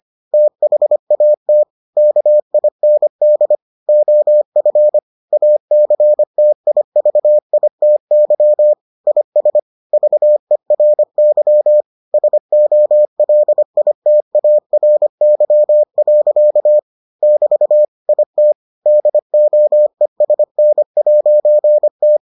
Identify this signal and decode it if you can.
Morse code, heard as a keyed tone, about 25 wpm, 600 Hz. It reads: THAT KIND OF ACTIVITY IS VERY SOLITARY. <BT> IT DOESN'T